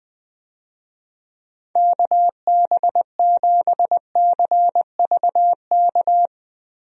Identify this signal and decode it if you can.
Morse code, heard as a keyed tone, about 20 wpm, 700 Hz. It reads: KB7CVK
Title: USA Call Signs - 6 Characters - Repeated 2x - 20wpm